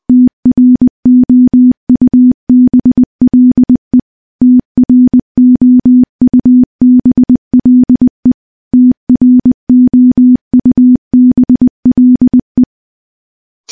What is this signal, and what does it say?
Morse code, heard as a keyed tone, about 21 wpm, 250 Hz. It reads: TROUBLE TROUBLE TROUBLE
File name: morse-training-20WPM